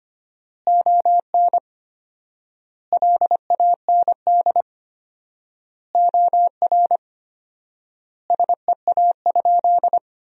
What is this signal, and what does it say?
Morse code, heard as a keyed tone, about 25 wpm, 700 Hz. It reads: ON LAND OR SEA?
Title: 4x Word-Spacing - Sentences from Top 500 Words - 25wpm